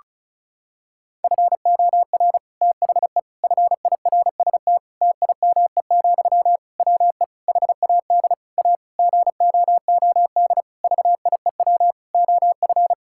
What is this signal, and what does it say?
Morse code, heard as a keyed tone, about 36 wpm, 700 Hz. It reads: FOR THE FIRST TIME, WE HAD A GOOD VIEW OF
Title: Sentences from Top 1000 Words - Review Part 6 - 35wpm